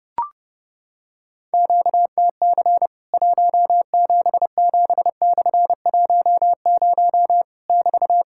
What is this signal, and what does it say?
Morse code, heard as a keyed tone, about 30 wpm, 700 Hz. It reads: QTC 177/10 <BT>
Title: Worked All Europe DX Contest - QTC Exchange - 30wpm